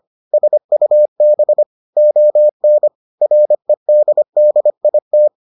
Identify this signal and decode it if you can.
Morse code, heard as a keyed tone, about 25 wpm, 600 Hz. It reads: SUB ON REDDIT